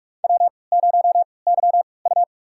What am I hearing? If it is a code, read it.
Morse code, heard as a keyed tone, about 46 wpm, spaced as for 25 wpm, 700 Hz. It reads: W0YU
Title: USA Call Signs - 3x Character Spacing - 45wpm